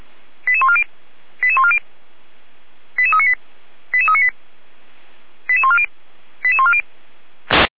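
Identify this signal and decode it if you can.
Five-tone selective call (ZVEI), heard as quick runs of five tones, tones of about 70 ms. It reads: 88140 88140 88398 88398 88140 88140
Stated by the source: ZVEI five-tone selective call 88398, received by SDR, from the audio.f85360.feuerwehr-orlamuende.org fire brigade server.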